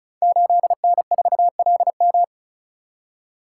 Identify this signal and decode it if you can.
Morse code, heard as a keyed tone, about 36 wpm, 700 Hz. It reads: 8N4LM
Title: Code Groups - 5 Letter:Numbers - 35wpm